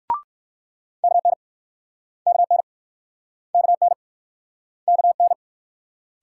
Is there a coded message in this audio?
Yes: Morse code from 1.0 to 5.3 s, heard as a keyed tone, about 60 wpm, 700 Hz. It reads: KN KN KN KN